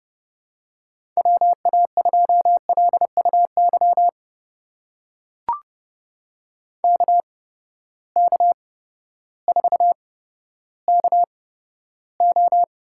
Morse code, heard as a keyed tone, about 30 wpm, spaced as for 14 wpm, 700 Hz. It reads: WA2LUY KK4KO
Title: USA Call Signs - 6 Characters - 8x Character Spacing - 30wpm